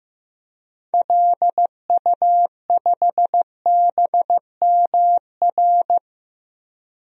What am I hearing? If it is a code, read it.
Morse code, heard as a keyed tone, about 15 wpm, 700 Hz. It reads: LU5BMR
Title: DXCC-100 Call Signs - 6 Characters - 15wpm